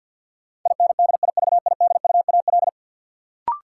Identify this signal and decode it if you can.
Morse code, heard as a keyed tone, about 47 wpm, 700 Hz. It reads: INDIVIDUAL
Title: Sentences from Top 1400 Words - New Word Drill Alt - 50wpm